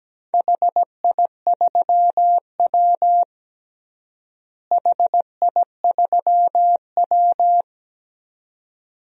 Morse code, heard as a keyed tone, about 17 wpm, 700 Hz. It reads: HI3W HI3W
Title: DXCC-100 Call Signs - 4 Characters - Repeated 2x - 17wpm